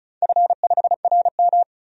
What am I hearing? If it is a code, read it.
Morse code, heard as a keyed tone, about 36 wpm, 700 Hz. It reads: F5RM
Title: DXCC-100 Call Signs - 4 Characters - 35wpm